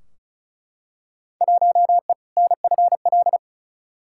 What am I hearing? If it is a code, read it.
Morse code, heard as a keyed tone, about 36 wpm, 700 Hz. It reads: 1E NFL
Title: Field Day Exchanges - 35wpm